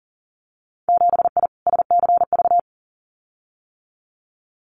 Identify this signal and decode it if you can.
Morse code, heard as a keyed tone, about 38 wpm, 700 Hz. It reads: 7I SCV
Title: Winter Field Day Exchanges Encore 40wpm